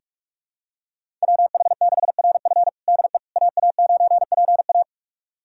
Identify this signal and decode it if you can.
Morse code, heard as a keyed tone, about 41 wpm, 700 Hz. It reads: WH6RF DE AA9PA
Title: USA Callsign DE Callsign - 5 Characters - 45wpm